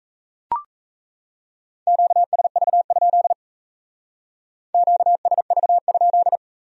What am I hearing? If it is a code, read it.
Morse code, heard as a keyed tone, about 35 wpm, 700 Hz. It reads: QSV? QSV?